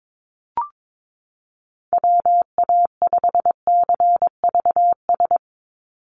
Morse code, heard as a keyed tone, about 22 wpm, 700 Hz. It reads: WA5CVS
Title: USA Call Signs 22wpm